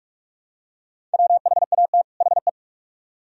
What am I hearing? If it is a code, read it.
Morse code, heard as a keyed tone, about 42 wpm, 700 Hz. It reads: WHAT HE